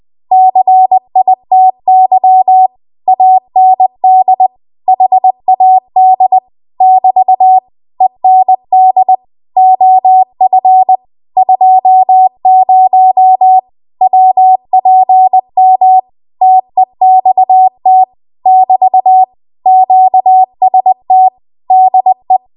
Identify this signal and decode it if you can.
Morse code, heard as a keyed tone, about 20 wpm, 750 Hz. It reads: CITY AND HAD <BT> END OF 20 WPM TEXT <BT> QST DE
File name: Morse / 20WPM